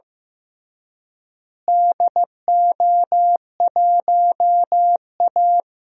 Morse code, heard as a keyed tone, about 16 wpm, 700 Hz. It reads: DO1A
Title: DXCC-100 Call Signs - 4 Characters - Repeated 3x - 15wpm